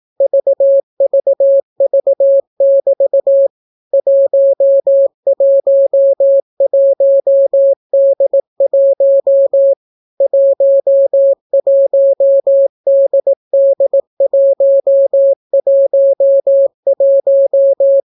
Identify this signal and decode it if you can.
Morse code, heard as a keyed tone, about 18 wpm, 550 Hz. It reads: VVV<BT> 111D1 11DD111